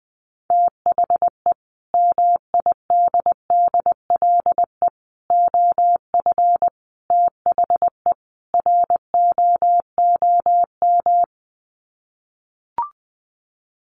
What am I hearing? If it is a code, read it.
Morse code, heard as a keyed tone, about 20 wpm, 700 Hz. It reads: THE MIDDLE OF THE ROOM
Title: Sets of 5 Words Encore2 20wpm